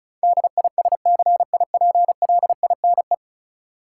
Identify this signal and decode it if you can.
Morse code, heard as a keyed tone, about 34 wpm, 700 Hz. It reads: DISCIPLINE
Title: Sentences from Top 900 Words - New Word Drill - 35wpm